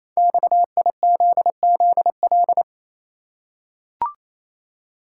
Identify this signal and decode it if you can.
Morse code, heard as a keyed tone, about 27 wpm, 700 Hz. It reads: XIZZL
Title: Code Groups - 5 Letters - 28wpm